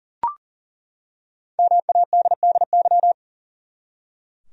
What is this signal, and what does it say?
Morse code, heard as a keyed tone, about 40 wpm, 700 Hz. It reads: MADDY
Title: USA Common Names - Max 5 Letters - 40wpm